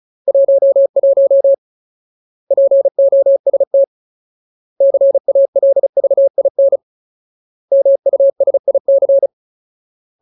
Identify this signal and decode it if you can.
Morse code, heard as a keyed tone, about 34 wpm, 550 Hz. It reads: 11 POST CALVIN MUSIC
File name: Ultimate Word List / 35WPM Ultimate